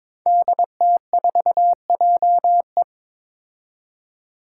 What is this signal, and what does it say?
Morse code, heard as a keyed tone, about 22 wpm, 700 Hz. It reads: DT4JE
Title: Five Challenging Letter-Number Combinations 22wpm